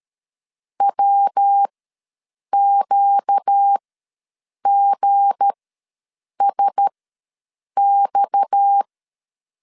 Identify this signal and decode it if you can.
Morse code, heard as a keyed tone, about 13 wpm, spaced as for 7 wpm, 800 Hz. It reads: WQGSX
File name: LEZIONI / 8 YJ45 / 8 YJ ws 12wpm 3342